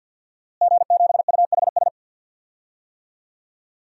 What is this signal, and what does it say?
Morse code, heard as a keyed tone, about 46 wpm, 700 Hz. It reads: G7UHS
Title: DXCC-100 Call Signs - 5 Characters - Repeated 3x - 50wpm